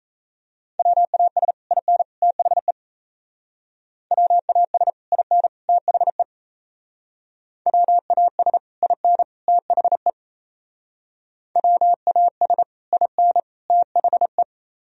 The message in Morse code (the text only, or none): WAS IN THE WAS IN THE WAS IN THE WAS IN THE